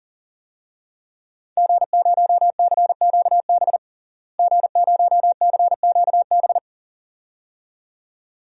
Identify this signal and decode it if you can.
Morse code, heard as a keyed tone, about 40 wpm, 700 Hz. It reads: G0CQB G0CQB
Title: DXCC-100 Call Signs - 5 Characters - Repeated 2x - 40wpm